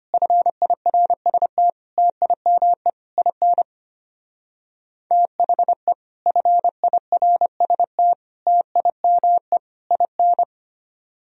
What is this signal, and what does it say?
Morse code, heard as a keyed tone, about 32 wpm, 700 Hz. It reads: FIRST TIME IN THE FIRST TIME IN